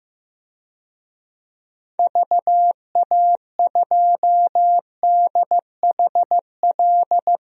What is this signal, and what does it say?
Morse code, heard as a keyed tone, about 15 wpm, 700 Hz. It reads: VA2DHL
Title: DXCC-100 Call Signs - 6 Characters - 15wpm